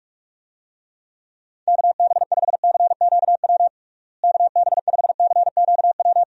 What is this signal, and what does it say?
Morse code, heard as a keyed tone, about 46 wpm, 700 Hz. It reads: KB5CQW KB5CQW
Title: USA Call Signs - 6 Characters - Repeated 2x - 45wpm